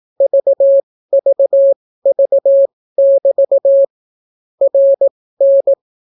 Morse code, heard as a keyed tone, about 18 wpm, spaced as for 14 wpm, 550 Hz. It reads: VVV<BT> RN